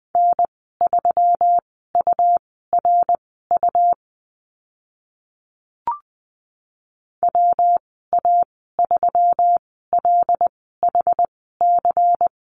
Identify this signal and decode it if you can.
Morse code, heard as a keyed tone, about 20 wpm, spaced as for 15 wpm, 700 Hz. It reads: N3URU WA3LHC
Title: 140 Lesson 23g - Letter f - Call Signs 20wpm